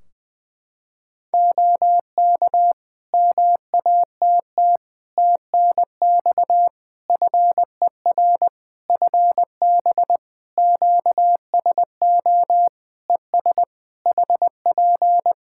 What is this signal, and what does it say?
Morse code, heard as a keyed tone, about 20 wpm, 700 Hz. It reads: OK MATT TNX FER FB QSO ES HP
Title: Practice QSO - Exchange 4 - Ending - 20wpm